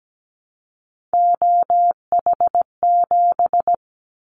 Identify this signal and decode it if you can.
Morse code, heard as a keyed tone, about 17 wpm, 700 Hz. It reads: OH7
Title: Callsign Prefixes for Top 25 DXCC - Repeated 3x - 17wpm